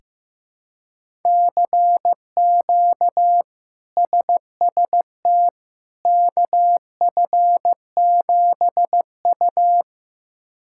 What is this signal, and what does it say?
Morse code, heard as a keyed tone, about 15 wpm, 700 Hz. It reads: CQ SST KF7U
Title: K1USN SST QSO Practice - 15wpm